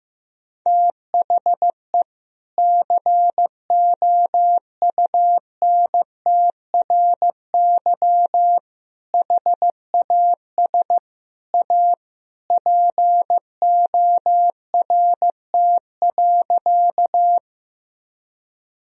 Morse code, heard as a keyed tone, about 15 wpm, 700 Hz. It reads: THE COUNTRY HAS A PORT.
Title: Sentences from Top 1000 Words - Review Part 1 - 15wpm